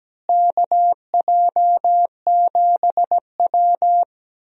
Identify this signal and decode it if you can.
Morse code, heard as a keyed tone, about 17 wpm, 700 Hz. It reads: KJ7W
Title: USA Call Signs - 4 Characters - Repeated 2x - 17wpm